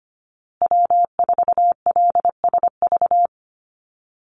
Morse code, heard as a keyed tone, about 25 wpm, 700 Hz. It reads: W4LSV
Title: Call Signs Repeated 2x - 25wpm